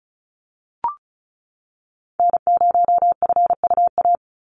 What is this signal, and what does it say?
Morse code, heard as a keyed tone, about 34 wpm, 700 Hz. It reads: N0FUA
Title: USA Call Signs 35wpm